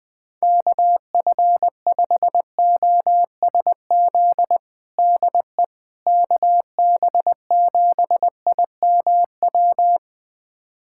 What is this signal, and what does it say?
Morse code, heard as a keyed tone, about 20 wpm, 700 Hz. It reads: KF5OSZ DE KB7IMW
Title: USA Callsign DE Callsign 20wpm